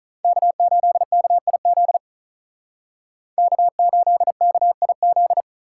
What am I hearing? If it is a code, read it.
Morse code, heard as a keyed tone, about 40 wpm, 700 Hz. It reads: K8KIZ K8KIZ
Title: Speed Racing - USA Call Signs 35wpm